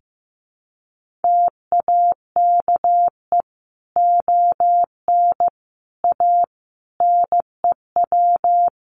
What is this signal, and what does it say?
Morse code, heard as a keyed tone, about 15 wpm, 700 Hz. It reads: TAKE ON A NEW
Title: Sentences from Top 1400 Words 15wpm